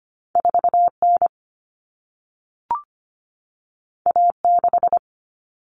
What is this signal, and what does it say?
Morse code, heard as a keyed tone, about 25 wpm, 700 Hz. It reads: 4N A6
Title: Common Cut Numbers - 2-Digit 25wpm